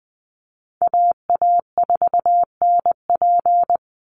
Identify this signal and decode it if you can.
Morse code, heard as a keyed tone, about 19 wpm, 700 Hz. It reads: AA4NP
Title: Call Signs Repeated 2x - 20wpm